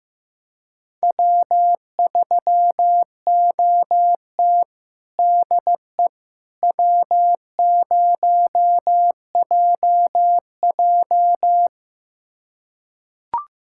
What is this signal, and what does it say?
Morse code, heard as a keyed tone, about 15 wpm, 700 Hz. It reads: W3OT DE W0JJ
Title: USA Callsign DE Callsign - 4 Characters - 15wpm